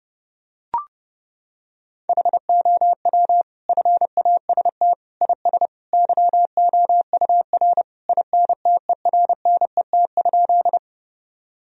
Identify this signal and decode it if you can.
Morse code, heard as a keyed tone, about 32 wpm, 700 Hz. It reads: HOW FAST IS YOUR INTERNET?
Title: Sentences from Top 1000 Words - Review Part 4 - 30wpm